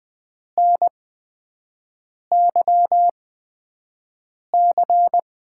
Morse code, heard as a keyed tone, about 20 wpm, spaced as for 5 wpm, 700 Hz. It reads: NYC
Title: USA Call Signs - 6 Characters - 8x Character Spacing - 20wpm